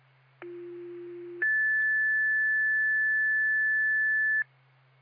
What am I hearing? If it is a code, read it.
Two-tone page, heard as a one-second tone, then a three-second tone, about 350 then 1750 Hz. It is heard from 0.4 to 4.4 s.